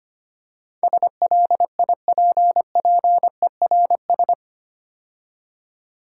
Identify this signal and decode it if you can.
Morse code, heard as a keyed tone, about 27 wpm, 700 Hz. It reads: SLIPPERS
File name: practice / 25wpm / Difficult Words - Speed-Racing with Hints - 25wpm